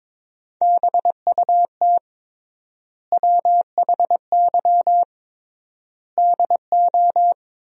Morse code, heard as a keyed tone, about 22 wpm, 700 Hz. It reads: BUT WHY DO
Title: 3x Word-Spacing - Sentences from Top 200 Words - 22wpm